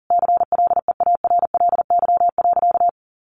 Morse code, heard as a keyed tone, about 37 wpm, 700 Hz. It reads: CLEARLY.
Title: Sentences from Top 1300 Words 40wpm